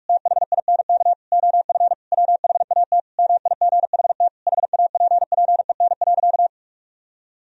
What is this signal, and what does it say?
Morse code, heard as a keyed tone, about 46 wpm, 700 Hz. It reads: THINK OF WHAT MIGHT HAPPEN.
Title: Sentences from Top 400 Words and Abbreviations 45wpm